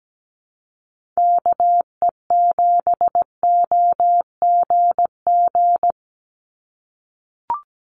Morse code, heard as a keyed tone, about 17 wpm, 700 Hz. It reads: KE7OGG
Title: USA Call Signs 17wpm